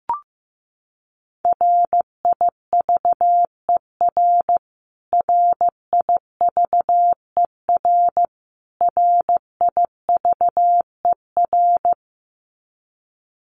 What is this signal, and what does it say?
Morse code, heard as a keyed tone, about 15 wpm, 700 Hz. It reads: RIVER RIVER RIVER
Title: Top 500 Words Repeated 15wpm